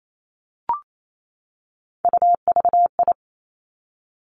Morse code, heard as a keyed tone, about 28 wpm, 700 Hz. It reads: UVI
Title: Three Challenging Letter-Number Combinations 28wpm